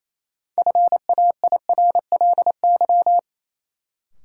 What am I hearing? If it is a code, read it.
Morse code, heard as a keyed tone, about 28 wpm, 700 Hz. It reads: FAIRLY
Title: Sentences from Top 1100 Words - New Word Drill - 28wpm